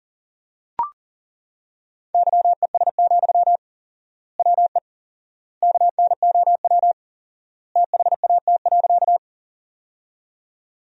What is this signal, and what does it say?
Morse code, heard as a keyed tone, about 40 wpm, 700 Hz. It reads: YES, WE KNOW THAT.
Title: 4x Word-Spacing - Sentences from Top 500 Words - 40wpm